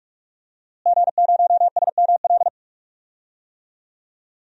Morse code, heard as a keyed tone, about 43 wpm, 700 Hz. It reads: G0SML
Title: DXCC-100 Call Signs - 5 Characters - Repeated 2x - 45wpm